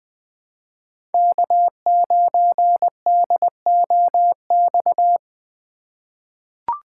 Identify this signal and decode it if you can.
Morse code, heard as a keyed tone, about 21 wpm, 700 Hz. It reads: K9DOX
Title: USA Call Signs - 5 Characters - 3x Character Spacing - 20wpm